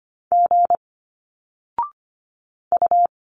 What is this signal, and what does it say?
Morse code, heard as a keyed tone, about 25 wpm, 700 Hz. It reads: G U
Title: Single Letters 25wpm